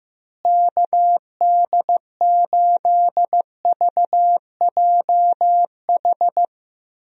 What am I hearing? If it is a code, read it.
Morse code, heard as a keyed tone, about 15 wpm, 700 Hz. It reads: KD8VJH
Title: USA Call Signs - 6 Characters - Repeated 2x - 15wpm